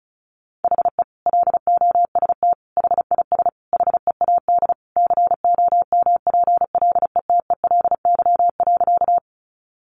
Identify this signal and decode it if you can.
Morse code, heard as a keyed tone, about 34 wpm, 700 Hz. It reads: HE LOST HIS HEAD COMPLETELY.
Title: Sentences from Top 1400 Words 35wpm